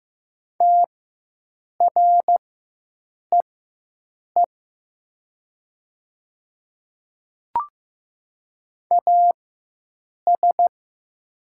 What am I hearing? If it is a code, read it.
Morse code, heard as a keyed tone, about 15 wpm, spaced as for 7 wpm, 700 Hz. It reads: TREE AS
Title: 4x Character Spacing - Top 500 Words - 15wpm